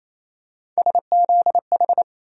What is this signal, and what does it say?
Morse code, heard as a keyed tone, about 28 wpm, 700 Hz. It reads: SZH
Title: USA Call Sign Postfixes - 3 Letters 28wpm